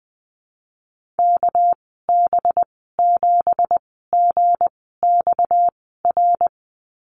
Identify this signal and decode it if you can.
Morse code, heard as a keyed tone, about 20 wpm, spaced as for 14 wpm, 700 Hz. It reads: KB7GXR
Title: 231 Lesson 37g - Letter x - Call Signs 20wpm